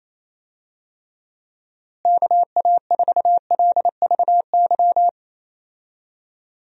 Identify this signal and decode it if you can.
Morse code, heard as a keyed tone, about 28 wpm, 700 Hz. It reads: KA4LVY
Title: USA Call Signs - 6 Characters - Repeated 2x - 28wpm